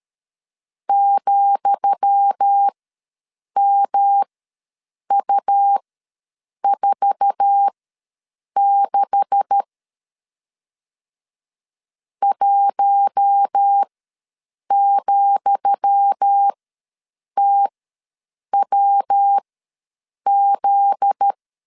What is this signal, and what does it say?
Morse code, heard as a keyed tone, about 13 wpm, spaced as for 7 wpm, 800 Hz. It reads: ,MU46 1,TWZ
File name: LEZIONI / 10 SIGNS / 10D COMA ws 12wpm 3618